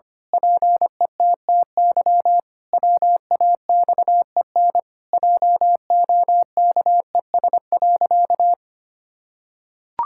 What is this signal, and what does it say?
Morse code, heard as a keyed tone, about 25 wpm, 700 Hz. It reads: PETTY WAXEN JOKES.
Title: Pangram Challenge v2 - 25wpm